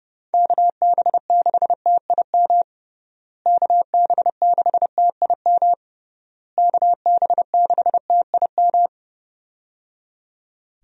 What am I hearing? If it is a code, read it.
Morse code, heard as a keyed tone, about 30 wpm, 700 Hz. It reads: KB6TIM KB6TIM KB6TIM
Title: USA Call Signs - 6 Characters - Repeated 3x - 30wpm